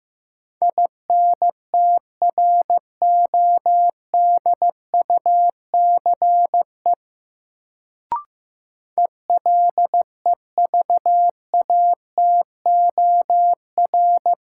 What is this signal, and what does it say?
Morse code, heard as a keyed tone, about 15 wpm, 700 Hz. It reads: INTRODUCE ELEVATOR
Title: Sentences from Top 1600 Words - New Word Drill Alt - 15wpm